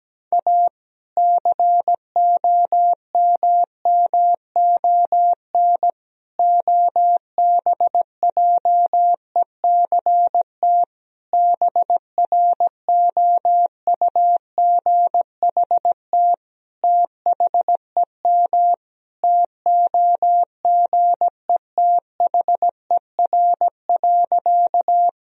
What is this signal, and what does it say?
Morse code, heard as a keyed tone, about 17 wpm, 700 Hz. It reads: A COMMON OBJECT BROUGHT THEM TOGETHER.
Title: Sentences from Top 1000 Words - Review Part 3 - 17wpm